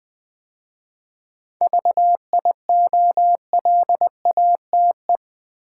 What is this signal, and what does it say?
Morse code, heard as a keyed tone, about 20 wpm, 700 Hz. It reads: VIOLATE
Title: 7 Letter Words - First 3 Letters Given - 20wpm